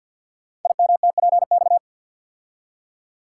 Morse code, heard as a keyed tone, about 50 wpm, 700 Hz. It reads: IMTPX
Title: Code Groups - 5 Letters - 50wpm